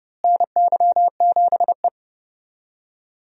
Morse code, heard as a keyed tone, about 30 wpm, 700 Hz. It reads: NY7E
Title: USA Call Signs - 4 Characters - Repeated 2x - 30wpm